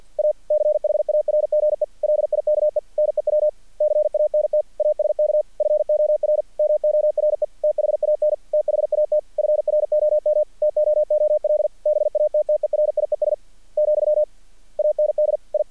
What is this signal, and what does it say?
Morse code, heard as a keyed tone, about 48 wpm, 600 Hz. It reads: A CHARGE LIKE NEW CANT ASK FOR MORE THAN THAT FROM TOOL BATTERIES , AND I
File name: archived / CRSnet